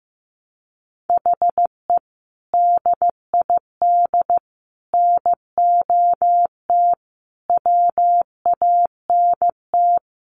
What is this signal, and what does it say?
Morse code, heard as a keyed tone, about 15 wpm, 700 Hz. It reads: HE DID NOT WANT
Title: Sets of 5 Words Encore 15wpm